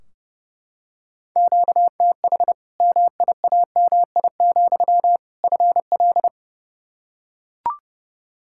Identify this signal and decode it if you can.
Morse code, heard as a keyed tone, about 30 wpm, 700 Hz. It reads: QTH MIAMI, FL
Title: Practice QSO - Exchange 1 - Beginning - 30wpm